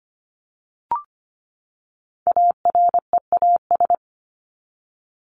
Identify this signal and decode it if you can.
Morse code, heard as a keyed tone, about 25 wpm, 700 Hz. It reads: AREAS
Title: ES:S Suffix - Top 100 Words with Base Words 25wpm